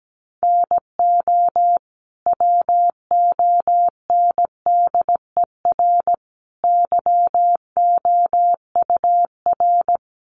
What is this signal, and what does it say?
Morse code, heard as a keyed tone, about 17 wpm, 700 Hz. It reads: NO WONDER YOUR
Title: Sentences from Top 1100 Words 17wpm